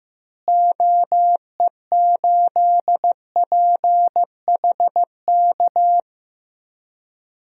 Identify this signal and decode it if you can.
Morse code, heard as a keyed tone, about 15 wpm, 700 Hz. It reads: OE8PHK
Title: DXCC-100 Call Signs - 6 Characters - Repeated 3x - 15wpm